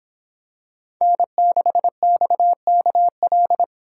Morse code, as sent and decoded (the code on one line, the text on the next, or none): -. -.... -..- -.- .-..
N6XKL